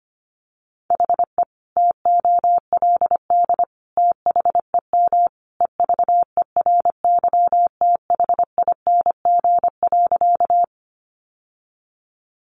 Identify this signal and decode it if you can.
Morse code, heard as a keyed tone, about 25 wpm, 700 Hz. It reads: HE TOLD THEM EVERYTHING.
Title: Sentences from Top 700 Words 25wpm